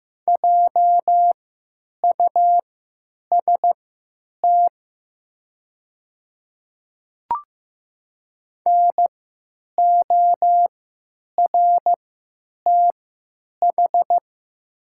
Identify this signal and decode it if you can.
Morse code, heard as a keyed tone, about 15 wpm, spaced as for 9 wpm, 700 Hz. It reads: JUST NORTH
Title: 3x Character Spacing - Top 500 Words - 15wpm